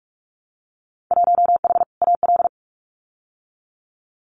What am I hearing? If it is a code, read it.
Morse code, heard as a keyed tone, about 46 wpm, 700 Hz. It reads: 1H AL
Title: Winter Field Day Exchanges 45wpm